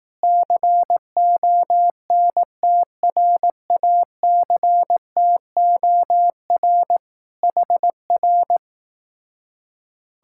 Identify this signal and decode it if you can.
Morse code, heard as a keyed tone, about 18 wpm, 700 Hz. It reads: CONTRACTOR HR
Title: Speed-Racing Common QSO Phrases Encore - 15wpm